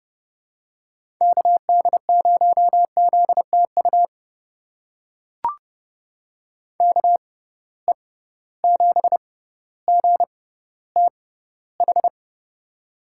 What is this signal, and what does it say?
Morse code, heard as a keyed tone, about 32 wpm, spaced as for 16 wpm, 700 Hz. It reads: KD0ZTU KE7GTH
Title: USA Call Signs - 6 Characters - 6x Character Spacing - 30wpm